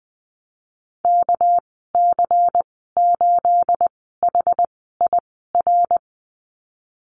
Morse code, heard as a keyed tone, about 20 wpm, spaced as for 15 wpm, 700 Hz. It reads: KC8HIR
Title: 238 Lesson 38g - Letter q - Call Signs 20wpm